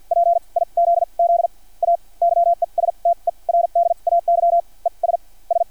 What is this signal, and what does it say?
Morse code, heard as a keyed tone, about 50 wpm, spaced as for 35 wpm, 690 Hz. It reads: PIZZA YESTERDAY ES S
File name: archived / CRSnet